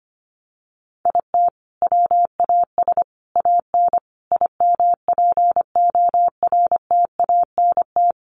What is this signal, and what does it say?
Morse code, heard as a keyed tone, about 25 wpm, 700 Hz. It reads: IT WAS AN IMPORTANT
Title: Sentences from Top 1400 Words 25wpm